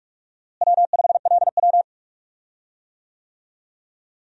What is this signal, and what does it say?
Morse code, heard as a keyed tone, about 47 wpm, 700 Hz. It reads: W5LW
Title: USA Call Signs - 4 Characters - Repeated 3x - 45wpm